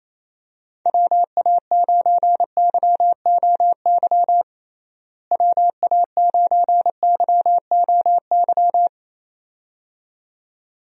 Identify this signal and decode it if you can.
Morse code, heard as a keyed tone, about 28 wpm, 700 Hz. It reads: WA9YOY WA9YOY
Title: USA Call Signs - 6 Characters - Repeated 2x - 28wpm